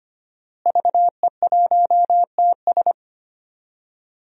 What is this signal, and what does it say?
Morse code, heard as a keyed tone, about 25 wpm, 700 Hz. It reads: VE1TS
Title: DXCC-100 Call Signs - 5 Characters - Repeated 3x - 25wpm